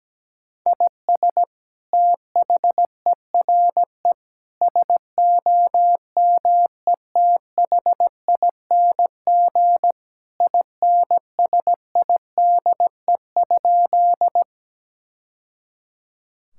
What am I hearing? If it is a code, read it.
Morse code, heard as a keyed tone, about 17 wpm, 700 Hz. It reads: IS THERE SOMETHING INSIDE?
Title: Sentences from Top 1000 Words - Review Part 6 - 17wpm